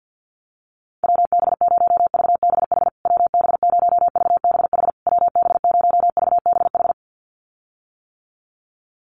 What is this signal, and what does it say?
Morse code, heard as a keyed tone, about 50 wpm, 700 Hz. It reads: WB0VBH WB0VBH WB0VBH
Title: USA Call Signs Repeated 3x 50wpm